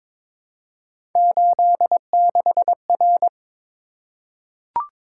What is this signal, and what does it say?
Morse code, heard as a keyed tone, about 23 wpm, 700 Hz. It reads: 86R
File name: practice / 22wpm / Three Letter-Number Combinations - 22wpm